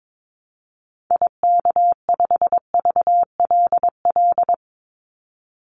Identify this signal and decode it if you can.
Morse code, heard as a keyed tone, about 22 wpm, 700 Hz. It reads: IK5VLL
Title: Call Signs - 22wpm